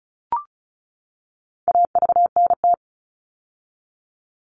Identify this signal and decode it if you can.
Morse code, heard as a keyed tone, about 34 wpm, 700 Hz. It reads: AVNT
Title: All Cut Numbers - 4-Digit 35wpm